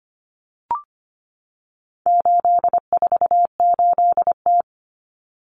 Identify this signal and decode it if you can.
Morse code, heard as a keyed tone, about 25 wpm, 700 Hz. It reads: 848T
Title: Common Cut Numbers - 4-Digit 25wpm